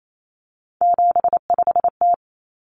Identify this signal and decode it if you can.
Morse code, heard as a keyed tone, about 28 wpm, 700 Hz. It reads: 75T
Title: Common Cut Numbers - 3-Digit 28wpm